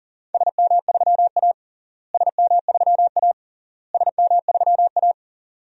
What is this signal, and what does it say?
Morse code, heard as a keyed tone, about 40 wpm, 700 Hz. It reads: SM3A SM3A SM3A
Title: DXCC-100 Call Signs - 4 Characters - Repeated 3x - 40wpm